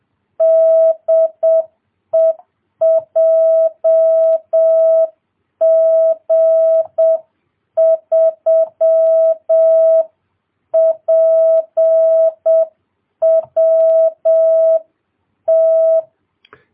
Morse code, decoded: DE JG3PWT

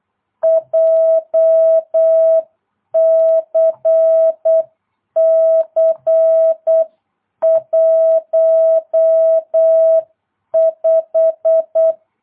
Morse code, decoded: JCC 15

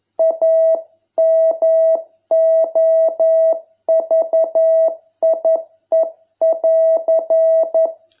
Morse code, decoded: A MOVIE<AR>

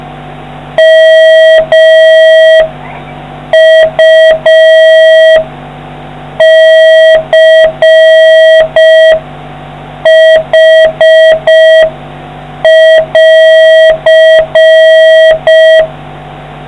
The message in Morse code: MUCH<AR>

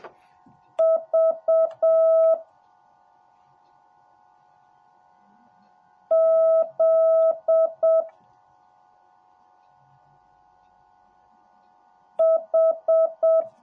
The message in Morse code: V Z H

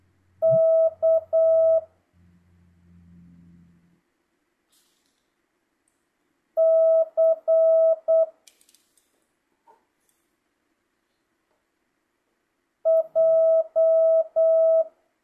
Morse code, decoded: K C J

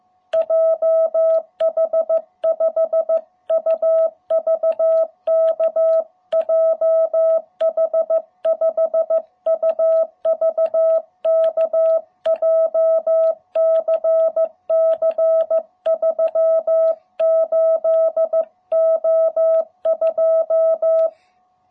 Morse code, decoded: JH5UVK JH5UVK JCC 38O2

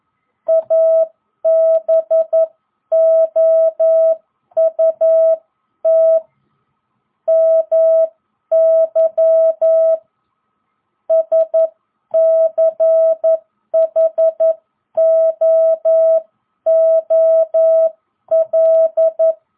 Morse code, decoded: ABOUT MY SCHOOL

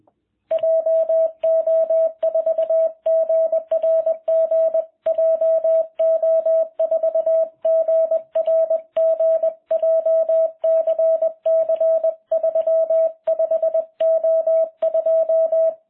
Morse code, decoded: JO4GRG JO4GRG JCC 35O2

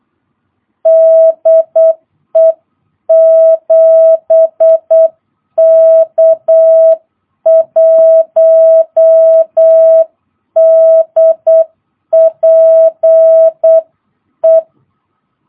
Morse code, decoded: DE 7K1DPE

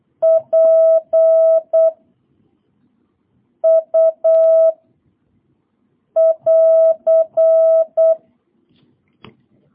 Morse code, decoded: P U <AR>

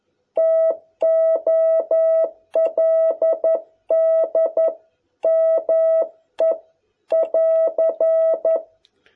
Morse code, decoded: TOLD ME <AR>

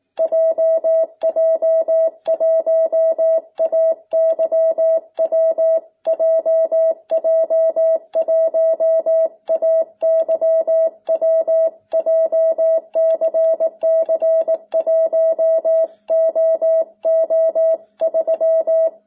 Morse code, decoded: JJ1AYW JJ1AYW JCC1OO3

